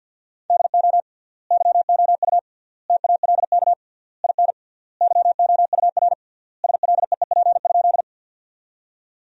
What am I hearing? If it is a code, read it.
Morse code, heard as a keyed tone, about 52 wpm, 700 Hz. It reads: DO YOU TALK IN YOUR SLEEP?
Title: 3x Word-Spacing - Sentences from Top 500 Words - 50wpm